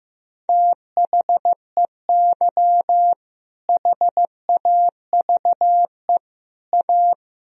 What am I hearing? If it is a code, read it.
Morse code, heard as a keyed tone, about 16 wpm, 700 Hz. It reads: THEY HAVE A
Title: Sentences from Top 1000 Words - Review Part 4 - 15wpm